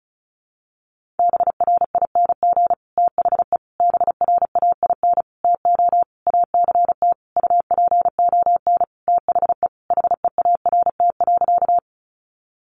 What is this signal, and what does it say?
Morse code, heard as a keyed tone, about 34 wpm, 700 Hz. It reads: BRING THE BRAIN TO ACT UPON THE HEART.
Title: Sentences from Top 1200 Words 35wpm